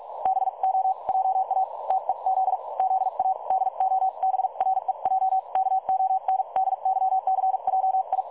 Morse code, heard as a keyed tone, about 46 wpm, spaced as for 39 wpm, 760 Hz. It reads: BO JA TEZ ZA DOBREJ KONDYCJI
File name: dlazaawansowanych / nagraniazpasma / websdr recording 3565.0kHz